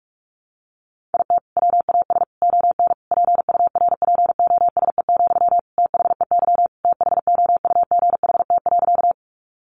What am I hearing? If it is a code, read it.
Morse code, heard as a keyed tone, about 44 wpm, 700 Hz. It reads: IT WAS ON PURPOSE, THEY THOUGHT.
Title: Sentences from Top 800 Words 45wpm